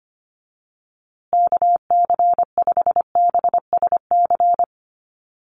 Morse code, heard as a keyed tone, about 25 wpm, 700 Hz. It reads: KC5BSC
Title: USA Call Signs Repeated 3x 25wpm